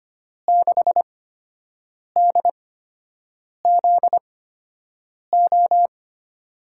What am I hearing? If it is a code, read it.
Morse code, heard as a keyed tone, about 25 wpm, spaced as for 7 wpm, 700 Hz. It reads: 6DZO
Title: USA Call Signs - 6 Characters - 8x Character Spacing - 25wpm